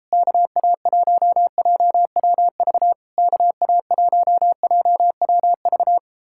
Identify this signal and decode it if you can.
Morse code, heard as a keyed tone, about 37 wpm, 700 Hz. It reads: KA1JWV KA1JWV